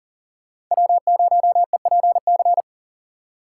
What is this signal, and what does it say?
Morse code, heard as a keyed tone, about 38 wpm, 700 Hz. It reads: W0EPC
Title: K1USN SST QSO Practice - 40wpm